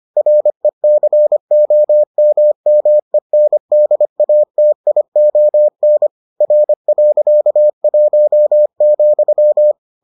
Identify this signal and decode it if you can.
Morse code, heard as a keyed tone, about 25 wpm, 600 Hz. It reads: RECOMMENDATION R.1,